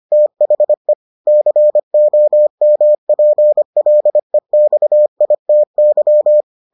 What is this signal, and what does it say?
Morse code, heard as a keyed tone, about 25 wpm, 600 Hz. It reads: THE COMPLEXITY